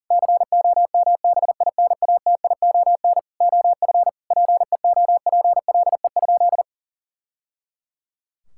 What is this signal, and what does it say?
Morse code, heard as a keyed tone, about 40 wpm, 700 Hz. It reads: COMBINATION OF PEOPLE?
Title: Sentences from Top 1000 Words - Review Part 6 - 40wpm